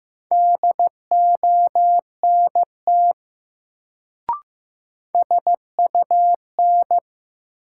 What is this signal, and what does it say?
Morse code, heard as a keyed tone, about 16 wpm, 700 Hz. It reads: DONT SUN
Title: Sentences from Top 300 Words - New Word Drill - 15wpm